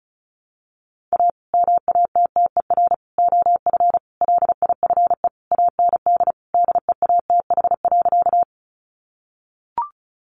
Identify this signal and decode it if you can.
Morse code, heard as a keyed tone, about 34 wpm, 700 Hz. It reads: A MATTER OF LIFE AND DEATH.
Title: Sentences from Top 1100 Words 35wpm